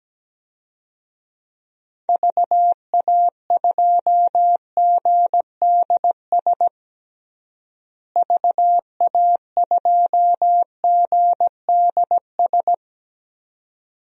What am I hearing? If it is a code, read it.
Morse code, heard as a keyed tone, about 17 wpm, 700 Hz. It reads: VA2GDS VA2GDS
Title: DXCC-100 Call Signs - 6 Characters - Repeated 2x - 17wpm